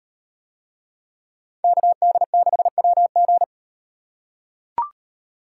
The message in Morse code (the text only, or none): KD6WG